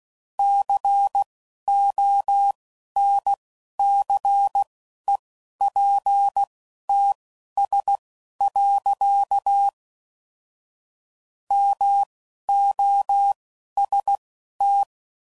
Morse code, heard as a keyed tone, about 16 wpm, spaced as for 11 wpm, 800 Hz. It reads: CONCEPTS. MOST
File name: CW Code Groups @10WPM / Technical Subjects / Gates Intro